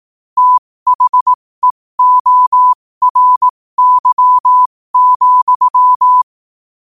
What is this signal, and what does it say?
Morse code, heard as a keyed tone, about 18 wpm, spaced as for 15 wpm, 1000 Hz. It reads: THEORY,
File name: qotdfiles / 15wpm